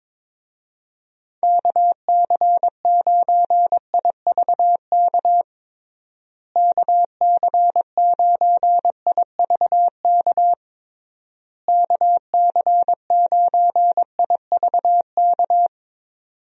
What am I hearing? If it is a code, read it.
Morse code, heard as a keyed tone, about 22 wpm, 700 Hz. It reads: KC9IVK KC9IVK KC9IVK
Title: USA Call Signs - 6 Characters - Repeated 3x - 22wpm